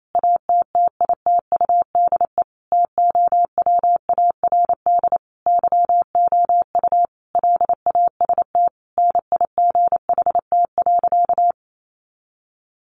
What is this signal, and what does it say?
Morse code, heard as a keyed tone, about 28 wpm, 700 Hz. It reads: ATTITUDE TOWARD YOU LAST NIGHT.
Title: Sentences from Top 1300 Words 28wpm